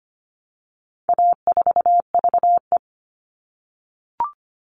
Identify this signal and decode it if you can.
Morse code, heard as a keyed tone, about 25 wpm, 700 Hz. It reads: A4VE
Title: All Cut Numbers - 4-Digit 25wpm